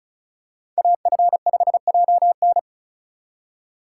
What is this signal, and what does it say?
Morse code, heard as a keyed tone, about 36 wpm, 700 Hz. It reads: AF5JN
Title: USA Call Signs - 5 Characters - 4x Character Spacing - 35wpm